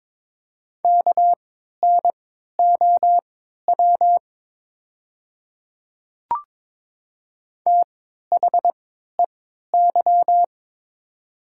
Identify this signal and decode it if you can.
Morse code, heard as a keyed tone, about 22 wpm, spaced as for 12 wpm, 700 Hz. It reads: KNOW THEY
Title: 3x Character Spacing - Top 100 Words - 22wpm